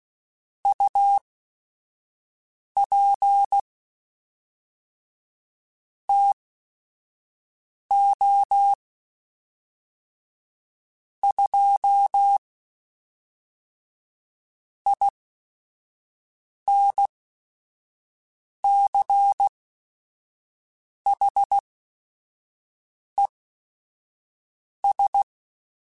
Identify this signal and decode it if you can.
Morse code, heard as a keyed tone, about 16 wpm, spaced as for 5 wpm, 800 Hz. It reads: UP TO 2 INCHES
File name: Weather Sets / Weather Sets-mp3s / Weather 3 17 @5